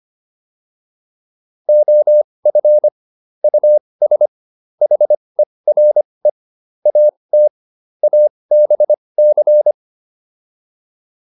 Morse code, heard as a keyed tone, about 25 wpm, spaced as for 20 wpm, 600 Hz. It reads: OF US HERE AT ABC
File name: WD405 20